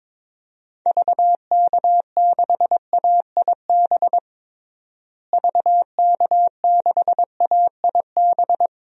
Morse code, heard as a keyed tone, about 21 wpm, 700 Hz. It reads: VK6AIB VK6AIB
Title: DXCC-100 Call Signs - 6 Characters - Repeated 2x - 22wpm